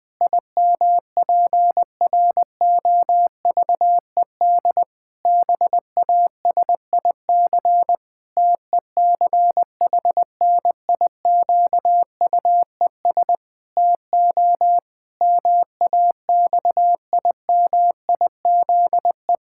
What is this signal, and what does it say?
Morse code, heard as a keyed tone, about 20 wpm, 700 Hz. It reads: IMPROVED BASIC TECHNIQUES TO MAXIMIZE
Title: Pangram Challenge v2 - 20wpm